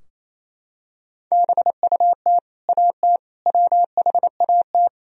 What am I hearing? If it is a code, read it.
Morse code, heard as a keyed tone, about 28 wpm, 700 Hz. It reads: BUT AT WHAT
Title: Sentences from Top 1000 Words - Review Part 4 - 28wpm